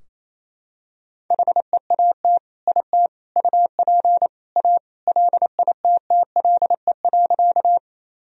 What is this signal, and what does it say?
Morse code, heard as a keyed tone, about 28 wpm, 700 Hz. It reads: HEAT IT UP A LITTLE.
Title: Sentences from Top 1000 Words - Review Part 3 - 28wpm